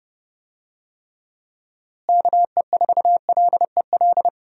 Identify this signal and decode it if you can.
Morse code, heard as a keyed tone, about 32 wpm, 700 Hz. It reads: KE4LEL
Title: USA Call Signs - 6 Characters - Repeated 3x - 30wpm